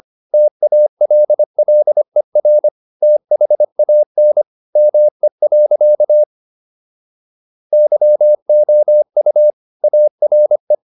Morse code, heard as a keyed tone, about 25 wpm, 600 Hz. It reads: TALLER THAN ME. YOU ARE
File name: SS123 25